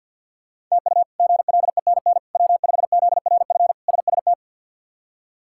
Morse code, heard as a keyed tone, about 50 wpm, 700 Hz. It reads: TU GLENN W5ZRF SST
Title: K1USN SST QSO Practice - 50wpm